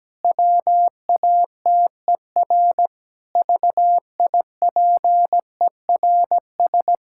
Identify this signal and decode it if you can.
Morse code, heard as a keyed tone, about 17 wpm, 700 Hz. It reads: WATER VIPERS
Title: Pangram Challenge v2 - 17wpm